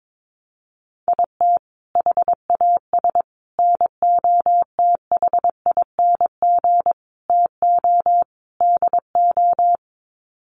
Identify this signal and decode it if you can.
Morse code, heard as a keyed tone, about 22 wpm, 700 Hz. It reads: IT HAS NOTHING TO DO
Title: Sets of 5 Words Encore 22wpm